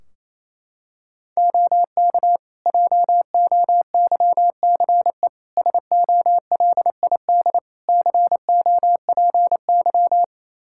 Morse code, heard as a keyed tone, about 28 wpm, 700 Hz. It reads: OK JOYCE SOLID COPY
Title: Practice QSO - Exchange 3 - More Info - 28wpm